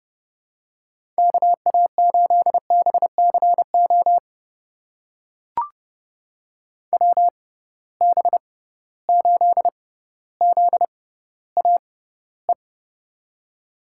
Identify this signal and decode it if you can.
Morse code, heard as a keyed tone, about 30 wpm, spaced as for 15 wpm, 700 Hz. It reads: KA8BCO WB8ZAE
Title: USA Call Signs - 6 Characters - 6x Character Spacing - 30wpm